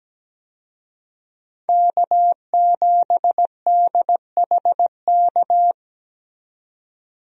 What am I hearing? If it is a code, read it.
Morse code, heard as a keyed tone, about 17 wpm, 700 Hz. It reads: K7DHK
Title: USA Call Signs - 5 Characters - 17wpm